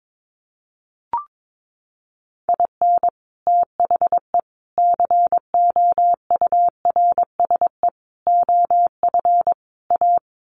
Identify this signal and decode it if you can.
Morse code, heard as a keyed tone, about 22 wpm, 700 Hz. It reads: IN THE COURSE OF A
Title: Sets of 5 Words Encore2 22wpm